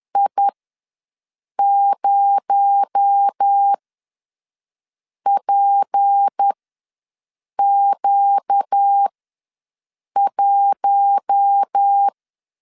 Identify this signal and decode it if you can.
Morse code, heard as a keyed tone, about 10 wpm, spaced as for 5 wpm, 800 Hz. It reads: I0PQ1
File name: LEZIONI / 6 PKX0 / 6 PKX0 ws 10wpm 4220